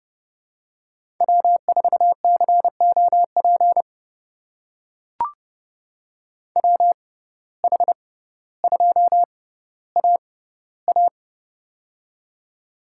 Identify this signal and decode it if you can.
Morse code, heard as a keyed tone, about 32 wpm, spaced as for 16 wpm, 700 Hz. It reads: W4COP WH2AA
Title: USA Call Signs - 6x Character Spacing - 30wpm